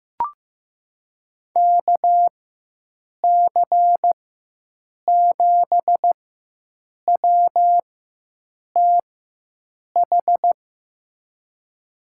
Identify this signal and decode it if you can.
Morse code, heard as a keyed tone, about 15 wpm, spaced as for 7 wpm, 700 Hz. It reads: KC7WTH
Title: USA Call Signs - 6 Characters - 4x Character Spacing - 15wpm